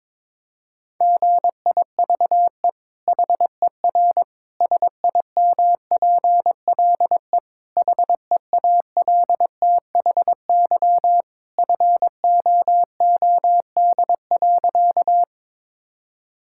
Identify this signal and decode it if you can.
Morse code, heard as a keyed tone, about 22 wpm, 700 Hz. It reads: GIVE HER SIMPLE HEALTHY FOOD.
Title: Sentences from Top 1000 Words - Review Part 5 - 22wpm